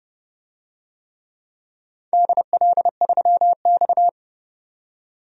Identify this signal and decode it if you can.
Morse code, heard as a keyed tone, about 30 wpm, 700 Hz. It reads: DL3X
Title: DXCC-100 Call Signs - 4 Characters - 30wpm